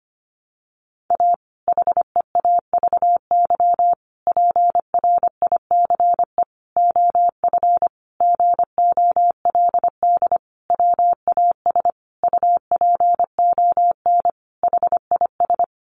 Morse code, heard as a keyed tone, about 25 wpm, 700 Hz. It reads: A HEAVY PRICE OF GOLD WAS UPON HIS
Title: Sentences from Top 1100 Words 25wpm